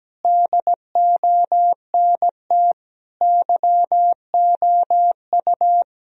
Morse code, heard as a keyed tone, about 17 wpm, 700 Hz. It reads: DONT YOU
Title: Sentences from Top 1000 Words - Review Part 4 - 17wpm